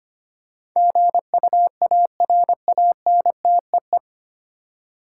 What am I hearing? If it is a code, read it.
Morse code, heard as a keyed tone, about 25 wpm, 700 Hz. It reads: GUARANTEE
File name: Nine Letter Words - 25wpm